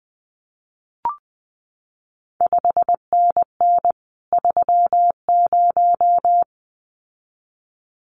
Morse code, heard as a keyed tone, about 20 wpm, 700 Hz. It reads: 5NN 30
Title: CQ WW DX Contest Exchange Encore - 20wpm